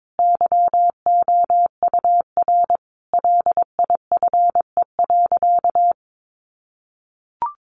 Morse code, heard as a keyed tone, about 22 wpm, 700 Hz. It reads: YOUR LIFE.
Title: Sentences from Top 700 Words 22wpm